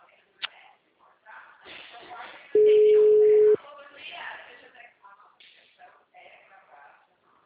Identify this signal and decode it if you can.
Ringback tone (Europe, 425 Hz), with speech.
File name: out- -2010- - - .14149